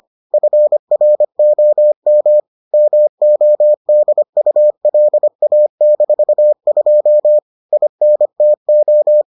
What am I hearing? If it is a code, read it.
Morse code, heard as a keyed tone, about 25 wpm, 600 Hz. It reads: FROM MODULA-2 INTO